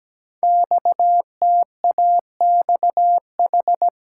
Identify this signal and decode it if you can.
Morse code, heard as a keyed tone, about 17 wpm, 700 Hz. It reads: XTAXH
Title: Code Groups - 5 Letters - 17wpm